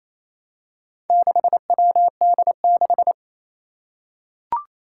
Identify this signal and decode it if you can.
Morse code, heard as a keyed tone, about 28 wpm, 700 Hz. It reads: 6WD6
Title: Four Challenging Letter-Number Combinations 28wpm